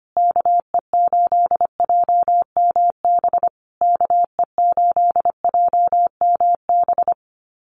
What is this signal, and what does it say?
Morse code, heard as a keyed tone, about 25 wpm, 700 Hz. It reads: KE8JMB KE8JMB
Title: USA Call Signs Repeated 3x 25wpm